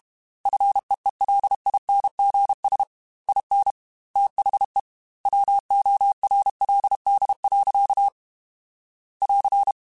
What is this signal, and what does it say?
Morse code, heard as a keyed tone, about 32 wpm, 800 Hz. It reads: FEELINGS IN THE WORLD. <AR>
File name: Quotes / Quotes-mp3s / ChildAsleep @30